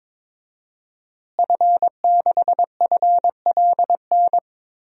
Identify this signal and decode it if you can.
Morse code, heard as a keyed tone, about 21 wpm, 700 Hz. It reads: F6FLN
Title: DXCC-100 Call Signs - 5 Characters - Repeated 3x - 22wpm